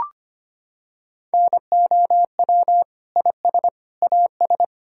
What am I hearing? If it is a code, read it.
Morse code, heard as a keyed tone, about 25 wpm, 700 Hz. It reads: NOW IS AS
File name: Sets of 9 Words - 25wpm